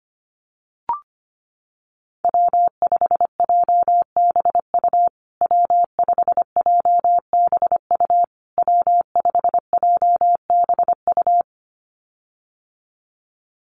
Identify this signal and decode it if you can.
Morse code, heard as a keyed tone, about 25 wpm, 700 Hz. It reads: W5JBU W5JBU W5JBU
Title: USA Call Signs Repeated 3x 25wpm